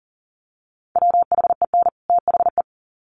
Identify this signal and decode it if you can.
Morse code, heard as a keyed tone, about 40 wpm, 700 Hz. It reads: WHEN THE
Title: Sets of 2 Words 40wpm